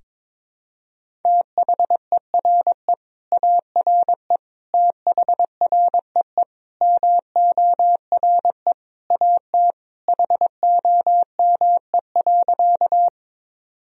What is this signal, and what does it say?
Morse code, heard as a keyed tone, about 22 wpm, 700 Hz. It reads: THERE ARE THREE MORE AT HOME.
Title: Sentences from Top 1000 Words - Review Part 1 - 22wpm